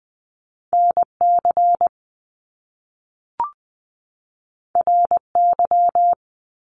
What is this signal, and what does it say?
Morse code, heard as a keyed tone, about 20 wpm, 700 Hz. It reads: NC RY
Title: Top Two Letter Combinations in Words 20wpm